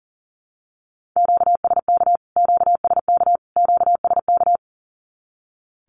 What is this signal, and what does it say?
Morse code, heard as a keyed tone, about 40 wpm, 700 Hz. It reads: QSK QSK QSK
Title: Q-Codes Repeated 3x 40wpm